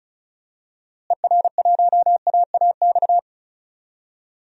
Morse code, heard as a keyed tone, about 36 wpm, 700 Hz. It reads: ER1AAX
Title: DXCC-100 Call Signs - 6 Characters - Repeated 2x - 35wpm